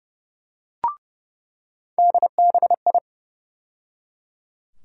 Morse code, heard as a keyed tone, about 30 wpm, 700 Hz. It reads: DBI
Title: USA Call Sign Postfixes - 3 Letters 30wpm